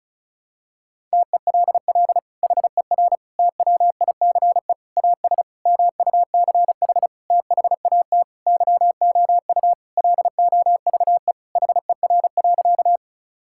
Morse code, heard as a keyed tone, about 36 wpm, 700 Hz. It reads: TELL HER TWICE AS MUCH THAT YOU LOVE HER.
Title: Sentences from Top 1600 Words 35wpm